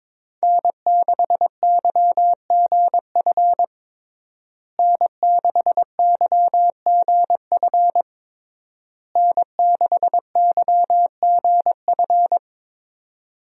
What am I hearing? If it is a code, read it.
Morse code, heard as a keyed tone, about 22 wpm, 700 Hz. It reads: N6YGF N6YGF N6YGF
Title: USA Call Signs - 5 Characters - Repeated 3x - 22wpm